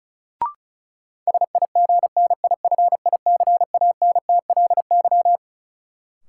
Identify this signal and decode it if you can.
Morse code, heard as a keyed tone, about 34 wpm, 700 Hz. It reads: SIGNIFICANTLY
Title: Sentences from Top 1500 Words - New Word Drill - 35wpm